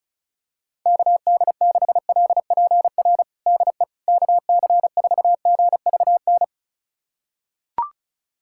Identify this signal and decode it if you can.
Morse code, heard as a keyed tone, about 34 wpm, 700 Hz. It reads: KD6LPR DE KC4GVN
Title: USA Callsign DE Callsign - 6 Characters - 35wpm